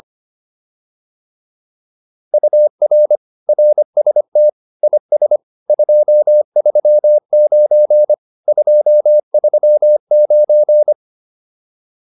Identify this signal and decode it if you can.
Morse code, heard as a keyed tone, about 25 wpm, 600 Hz. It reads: UR RST IS 239 239